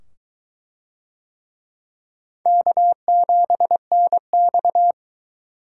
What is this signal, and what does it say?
Morse code, heard as a keyed tone, about 23 wpm, 700 Hz. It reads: K7NX